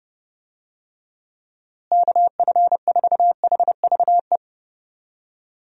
Morse code, heard as a keyed tone, about 30 wpm, 700 Hz. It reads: KF4HVE
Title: USA Call Signs - 6 Characters - Repeated 2x - 30wpm